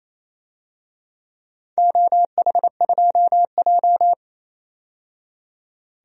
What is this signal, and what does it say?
Morse code, heard as a keyed tone, about 28 wpm, 700 Hz. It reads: OH2J
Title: DXCC-100 Call Signs - 4 Characters - 28wpm